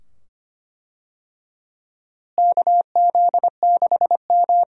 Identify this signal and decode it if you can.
Morse code, heard as a keyed tone, about 25 wpm, 700 Hz. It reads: KZ6M
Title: USA Call Signs - 4 Characters - Repeated 2x - 25wpm